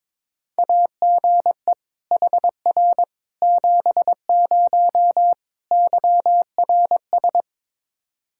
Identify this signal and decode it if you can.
Morse code, heard as a keyed tone, about 22 wpm, 700 Hz. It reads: AGE HR 70 YRS
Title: Practice QSO - Exchange 3 - More Info - 22wpm